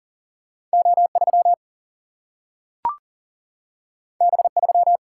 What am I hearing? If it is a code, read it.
Morse code, heard as a keyed tone, about 40 wpm, 700 Hz. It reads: O3 B3
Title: Single-Letter Single-Number 40wpm